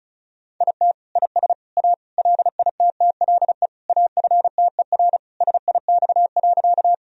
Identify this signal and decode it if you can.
Morse code, heard as a keyed tone, about 34 wpm, 700 Hz. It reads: IT IS A LITTLE AFTER SIX.
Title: Sentences from Top 1000 Words - Review Part 2 - 35wpm